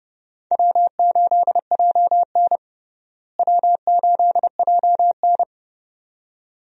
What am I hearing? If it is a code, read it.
Morse code, heard as a keyed tone, about 30 wpm, 700 Hz. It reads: W8JN W8JN
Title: USA Call Signs - 4 Characters - Repeated 2x - 30wpm